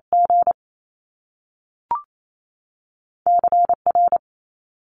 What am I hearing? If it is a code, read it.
Morse code, heard as a keyed tone, about 28 wpm, 700 Hz. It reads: G CR
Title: Top Two Letter Combinations in Words 28wpm